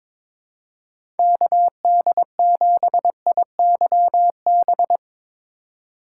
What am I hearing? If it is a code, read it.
Morse code, heard as a keyed tone, about 22 wpm, 700 Hz. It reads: KD7IYB
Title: USA Call Signs - 6 Characters - 4x Character Spacing - 22wpm